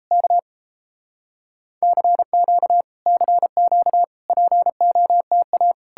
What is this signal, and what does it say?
Morse code, heard as a keyed tone, about 39 wpm, 700 Hz. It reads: K CQ CQ POTA